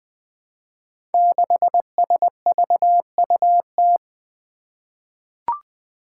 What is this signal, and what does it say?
Morse code, heard as a keyed tone, about 20 wpm, 700 Hz. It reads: 6SVUT
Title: Five Challenging Letter-Number Combinations 20wpm